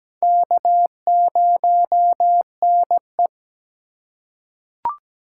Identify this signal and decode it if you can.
Morse code, heard as a keyed tone, about 17 wpm, 700 Hz. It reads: K0NE
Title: USA Call Signs - 8x Character Spacing - 17wpm